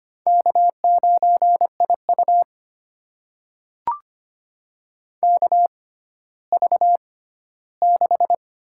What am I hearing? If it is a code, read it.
Morse code, heard as a keyed tone, about 25 wpm, spaced as for 14 wpm, 700 Hz. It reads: K9IU KV6
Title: USA Call Signs - 4 Characters - 6x Character Spacing - 25wpm